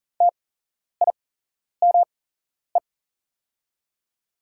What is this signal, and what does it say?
Morse code, heard as a keyed tone, about 40 wpm, spaced as for 11 wpm, 690 Hz. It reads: TIME
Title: 8x Character Spacing - Top 100 Words - 40wpm